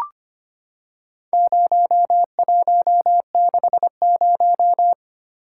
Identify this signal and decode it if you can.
Morse code, heard as a keyed tone, about 25 wpm, 700 Hz. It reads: 0160
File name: practice / 25wpm / Four Letter-Number Combinations - 25wpm